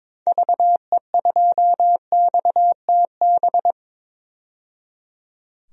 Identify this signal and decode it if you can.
Morse code, heard as a keyed tone, about 22 wpm, 700 Hz. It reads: VE2XTB
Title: DXCC-100 Call Signs - 6 Characters - 22wpm